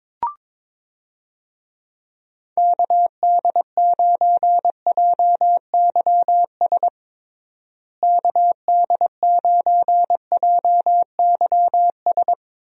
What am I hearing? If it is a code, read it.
Morse code, heard as a keyed tone, about 22 wpm, 700 Hz. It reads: KD9JYS KD9JYS
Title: USA Call Signs - 6 Characters - Repeated 3x - 22wpm